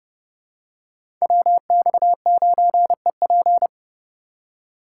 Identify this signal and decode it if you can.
Morse code, heard as a keyed tone, about 30 wpm, 700 Hz. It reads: WX9EP
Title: USA Call Signs - 5 Characters - 3x Character Spacing - 30wpm